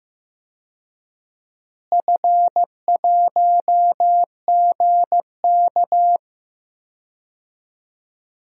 Morse code, heard as a keyed tone, about 15 wpm, 700 Hz. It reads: F1GK
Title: DXCC-100 Call Signs - 4 Characters - Repeated 3x - 15wpm